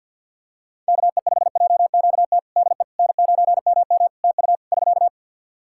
Morse code, heard as a keyed tone, about 51 wpm, 700 Hz. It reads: KE5JQT DE N9MM TU <SK>
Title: K1USN SST QSO Practice 50wpm